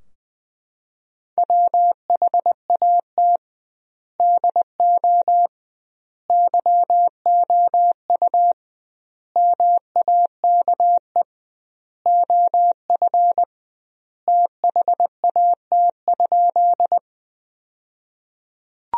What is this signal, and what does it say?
Morse code, heard as a keyed tone, about 21 wpm, 700 Hz. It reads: WHAT DO YOU MAKE OF THAT?
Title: 2x Word-Spacing - Sentences from Top 100 Words - 20wpm